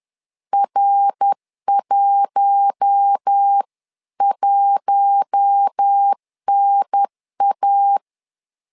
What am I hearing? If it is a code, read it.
Morse code, heard as a keyed tone, about 10 wpm, 800 Hz. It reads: R11NA
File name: LEZIONI / 4 Q1 10wpm 2448